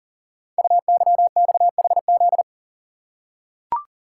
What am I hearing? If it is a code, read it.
Morse code, heard as a keyed tone, about 42 wpm, 690 Hz. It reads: UYXHZ
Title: Code Groups - 5 Letters - 40wpm